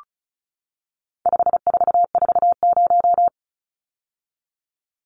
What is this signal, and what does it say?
Morse code, heard as a keyed tone, about 34 wpm, 700 Hz. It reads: 5440
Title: Numbers - 4-Digit 35wpm